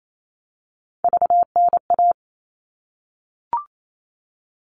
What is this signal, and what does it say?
Morse code, heard as a keyed tone, about 28 wpm, 700 Hz. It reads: VNA
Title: All Cut Numbers - 3-Digit 28wpm